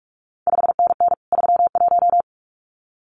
Morse code, heard as a keyed tone, about 46 wpm, 700 Hz. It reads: 5NN 31
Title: CQ WW DX Contest Exchange Encore - 45wpm